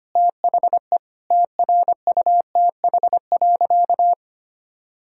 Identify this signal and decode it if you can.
Morse code, heard as a keyed tone, about 25 wpm, 700 Hz. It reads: THE TRUTH.
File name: practice / 25wpm / Sentences with 9-Word Phrases - 25wpm